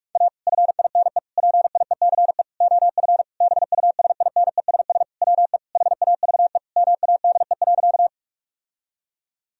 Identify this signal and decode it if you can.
Morse code, heard as a keyed tone, about 46 wpm, 700 Hz. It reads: A FINE PIECE OF BUSINESS WE HAVE MADE.
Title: Sentences from Top 1000 Words - Review Part 2 - 45wpm